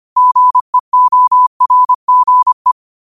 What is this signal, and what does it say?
Morse code, heard as a keyed tone, about 25 wpm, 1000 Hz. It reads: GEORGE
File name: qotdfiles / 25wpm